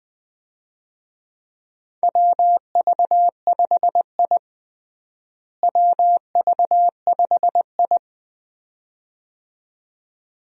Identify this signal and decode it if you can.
Morse code, heard as a keyed tone, about 21 wpm, 700 Hz. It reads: WV5I WV5I
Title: USA Call Signs - 4 Characters - Repeated 2x - 20wpm